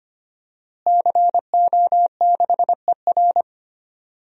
Morse code, heard as a keyed tone, about 25 wpm, 700 Hz. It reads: CO6ER
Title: DXCC-100 Call Signs - 5 Characters - Repeated 3x - 25wpm